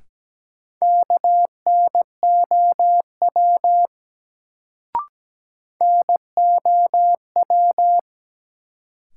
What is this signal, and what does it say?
Morse code, heard as a keyed tone, about 17 wpm, 700 Hz. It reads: KNOW NOW
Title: Sentences from Top 100 Words - New Word Drill - 17wpm